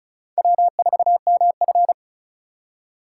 Morse code, heard as a keyed tone, about 36 wpm, 700 Hz. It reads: W4MF
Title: USA Call Signs - 4 Characters - 3x Character Spacing - 35wpm